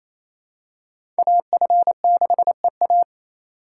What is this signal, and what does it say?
Morse code, heard as a keyed tone, about 28 wpm, 700 Hz. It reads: AF6EA
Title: USA Call Signs - 5 Characters - 4x Character Spacing - 28wpm